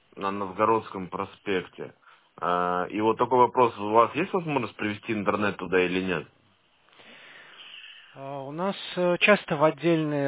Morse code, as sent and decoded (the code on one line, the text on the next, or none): none